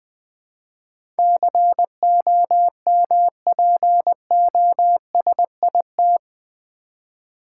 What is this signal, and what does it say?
Morse code, heard as a keyed tone, about 20 wpm, 700 Hz. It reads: COMPOSIT
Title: Build A Word - 4 Letters At A Time - 11-Letter Words - 20wpm